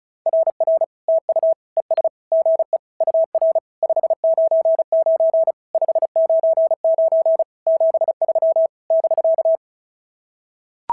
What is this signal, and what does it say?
Morse code, heard as a keyed tone, about 36 wpm, 690 Hz. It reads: RR TU ES GE UR 599 599 73 <BK>
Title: SOTA QSO Practice 35wpm